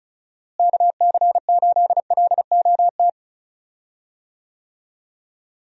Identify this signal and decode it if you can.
Morse code, heard as a keyed tone, about 36 wpm, 700 Hz. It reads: KC8LOT
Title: USA Call Signs - 6 Characters - Repeated 2x - 35wpm